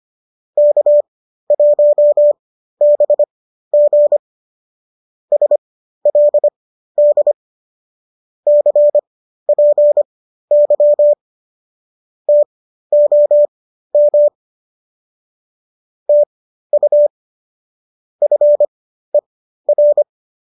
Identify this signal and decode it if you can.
Morse code, heard as a keyed tone, about 25 wpm, spaced as for 13 wpm, 590 Hz. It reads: K1BG SLD CPY TOM TU FER